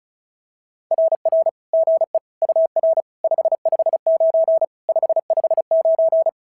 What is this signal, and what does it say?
Morse code, heard as a keyed tone, about 34 wpm, 650 Hz. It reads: RR GE UR 559 559
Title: SOTA QSO Practice 35wpm